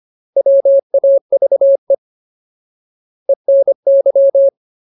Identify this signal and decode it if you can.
Morse code, heard as a keyed tone, about 25 wpm, 550 Hz. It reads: WAVE ENY
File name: Ultimate Word List / 25WPM Ultimate